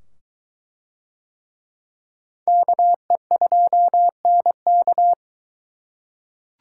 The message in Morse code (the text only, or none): KE2NK